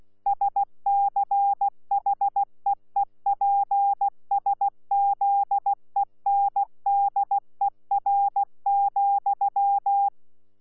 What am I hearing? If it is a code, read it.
Morse code, heard as a keyed tone, about 16 wpm, 800 Hz. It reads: SCHEEPSZENDER,
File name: Z PI4VRZ / PI4VRZ 14-jun-2025